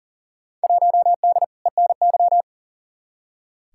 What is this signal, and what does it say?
Morse code, heard as a keyed tone, about 40 wpm, 700 Hz. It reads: 1D ENY
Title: Field Day Exchanges - 40wpm